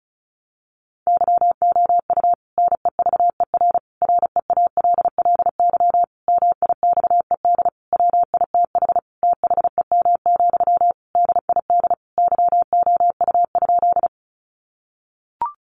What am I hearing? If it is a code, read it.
Morse code, heard as a keyed tone, about 34 wpm, 700 Hz. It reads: YOU NEVER REALLY MIXED WITH THEM, DID YOU?
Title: Sentences from Top 1500 Words 35wpm